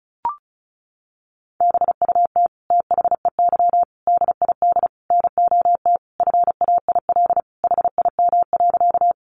Morse code, heard as a keyed tone, about 34 wpm, 700 Hz. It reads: BUT THEY DID NOT FAIL HIM.
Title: Sentences from Top 1100 Words 35wpm